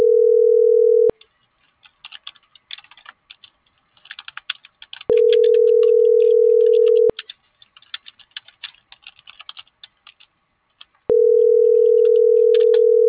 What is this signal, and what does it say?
Ringback tone (North America), 440 + 480 Hz.